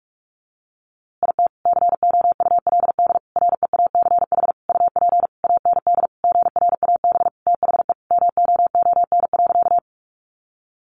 Morse code, heard as a keyed tone, about 44 wpm, 700 Hz. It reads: IT COULD REACH UP AND GRAB THE MOON.
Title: A Sound Of Thunder 45wpm